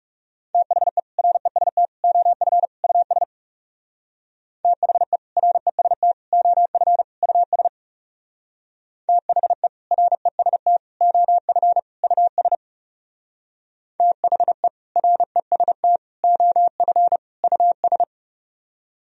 Morse code, heard as a keyed tone, about 39 wpm, 700 Hz. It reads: THE REST OF US THE REST OF US THE REST OF US THE REST OF US